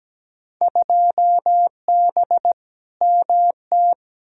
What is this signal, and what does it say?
Morse code, heard as a keyed tone, about 17 wpm, 700 Hz. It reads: 2B MT
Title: Field Day Exchanges Encore - 17wpm